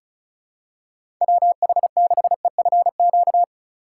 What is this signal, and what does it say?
Morse code, heard as a keyed tone, about 36 wpm, 700 Hz. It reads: WH6EFQ
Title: USA Call Signs - 6 Characters - 3x Character Spacing - 35wpm